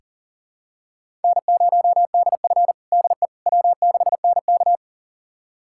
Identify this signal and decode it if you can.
Morse code, heard as a keyed tone, about 40 wpm, 700 Hz. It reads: N0DF DE W6NK
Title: USA Callsign DE Callsign - 4 Characters - 40wpm